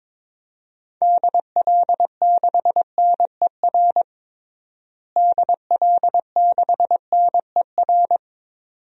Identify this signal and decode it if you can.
Morse code, heard as a keyed tone, about 22 wpm, 700 Hz. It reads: DL6NER DL6NER
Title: DXCC-100 Call Signs - 6 Characters - Repeated 3x - 22wpm